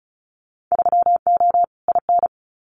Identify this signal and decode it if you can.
Morse code, heard as a keyed tone, about 36 wpm, 700 Hz. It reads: 3O IN
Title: Winter Field Day Exchanges Encore 35wpm